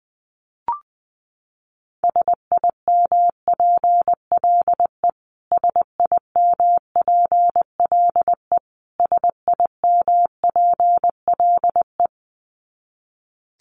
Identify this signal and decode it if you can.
Morse code, heard as a keyed tone, about 20 wpm, 700 Hz. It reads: SIMPLE SIMPLE SIMPLE
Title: Top 500 Words Repeated 20wpm